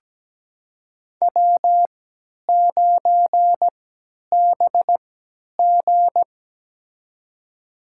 Morse code, heard as a keyed tone, about 17 wpm, spaced as for 10 wpm, 700 Hz. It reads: W9BG
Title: USA Call Signs - 3x Character Spacing - 17wpm